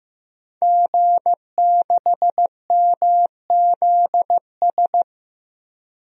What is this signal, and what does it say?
Morse code, heard as a keyed tone, about 15 wpm, 700 Hz. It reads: G6MZS
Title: DXCC-100 Call Signs - 5 Characters - Repeated 2x - 15wpm